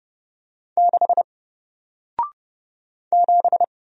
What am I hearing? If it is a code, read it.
Morse code, heard as a keyed tone, about 30 wpm, 700 Hz. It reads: 6 7
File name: Single-Letter-Number-30wpm